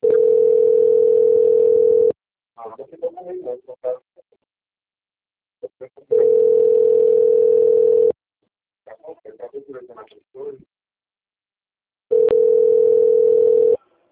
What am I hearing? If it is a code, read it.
Ringback tone (North America), 440 + 480 Hz, with speech.
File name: out- - - - - .10716